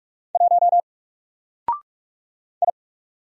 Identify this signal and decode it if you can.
Morse code, heard as a keyed tone, about 45 wpm, 700 Hz. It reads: 1 I
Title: Single Letter-Number Alt 45wpm